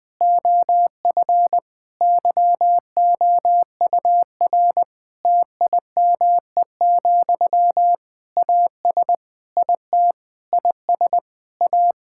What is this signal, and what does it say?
Morse code, heard as a keyed tone, about 20 wpm, 700 Hz. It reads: OF YOUR TIME, AS IT IS A
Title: Difficult Sentences with Hints - 20wpm